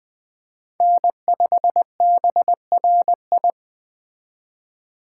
Morse code, heard as a keyed tone, about 21 wpm, 700 Hz. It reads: N5BRI
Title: Speed Racing - USA 1x3 Call Signs 17wpm